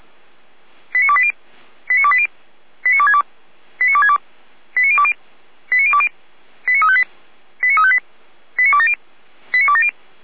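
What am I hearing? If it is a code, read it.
Five-tone selective call (ZVEI), heard as quick runs of five tones, tones of about 70 ms. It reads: 89280 89280 89262 89262 89020 89020 89358 89358 89270 89270
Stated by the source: ZVEI five-tone selective call received by SDR, logged by feuerwehr-orlamuende.org (fire brigade Orlamünde).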